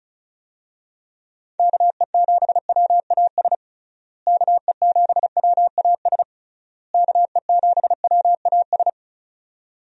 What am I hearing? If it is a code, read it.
Morse code, heard as a keyed tone, about 36 wpm, 700 Hz. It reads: KE7WAS KE7WAS KE7WAS
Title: USA Call Signs - 6 Characters - Repeated 3x - 35wpm